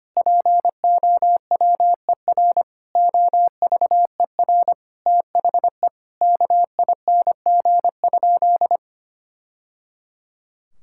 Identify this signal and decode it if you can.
Morse code, heard as a keyed tone, about 25 wpm, 700 Hz. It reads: POWER OVER THE KING?
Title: Sentences from Top 1000 Words - Review Part 6 - 25wpm